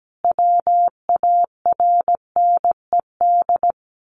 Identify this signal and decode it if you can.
Morse code, heard as a keyed tone, about 17 wpm, 700 Hz. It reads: WARNED
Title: A Sound Of Thunder 17wpm